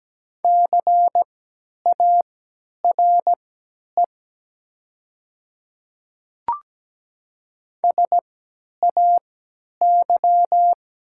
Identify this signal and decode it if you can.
Morse code, heard as a keyed tone, about 17 wpm, spaced as for 10 wpm, 700 Hz. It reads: CARE SAY
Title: 3x Character Spacing - Top 300 Words - 17wpm